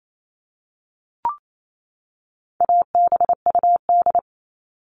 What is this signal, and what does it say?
Morse code, heard as a keyed tone, about 28 wpm, 700 Hz. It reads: ABUD
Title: All Cut Numbers - 4-Digit 28wpm